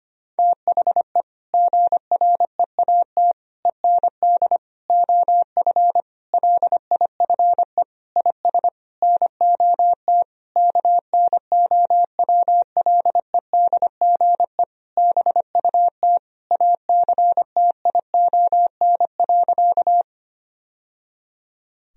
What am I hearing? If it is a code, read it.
Morse code, heard as a keyed tone, about 25 wpm, 700 Hz. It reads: THE GREAT END OF LIFE IS NOT KNOWLEDGE BUT ACTION.
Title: Sentences from Top 1000 Words - Review Part 4 - 25wpm